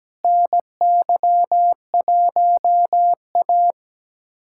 Morse code, heard as a keyed tone, about 17 wpm, 700 Hz. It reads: NY1A
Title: USA Callsign DE Callsign - 4 Characters - 17wpm